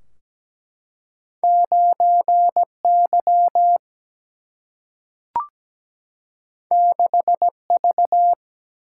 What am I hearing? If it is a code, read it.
Morse code, heard as a keyed tone, about 17 wpm, 700 Hz. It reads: 9 Y 6 V
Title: Single-Letter Single-Number 17wpm